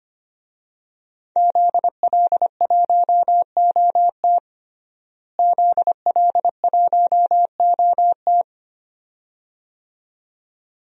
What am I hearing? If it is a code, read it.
Morse code, heard as a keyed tone, about 25 wpm, 700 Hz. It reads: ZL1OT ZL1OT
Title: DXCC-100 Call Signs - 5 Characters - Repeated 2x - 25wpm